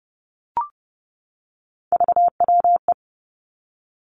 Morse code, heard as a keyed tone, about 30 wpm, 700 Hz. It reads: VWE
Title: Three Challenging Letter-Number Combinations 30wpm